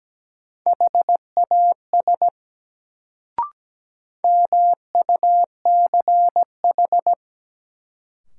Morse code, heard as a keyed tone, about 17 wpm, 700 Hz. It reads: HAS MUCH
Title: Sets of 2 Words - New Word Drill - 17wpm